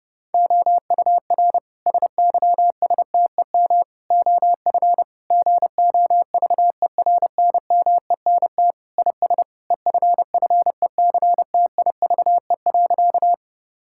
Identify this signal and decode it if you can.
Morse code, heard as a keyed tone, about 30 wpm, 700 Hz. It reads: OUR SYSTEM OF GOVERNMENT IS EFFECTIVE.
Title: Sentences from Top 1000 Words - Review Part 6 - 30wpm